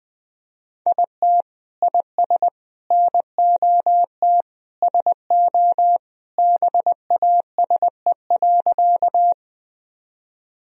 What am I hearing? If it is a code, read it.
Morse code, heard as a keyed tone, about 21 wpm, 700 Hz. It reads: IT IS NOT SO BASE.
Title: Sentences from Top 1000 Words - Review Part 1 - 20wpm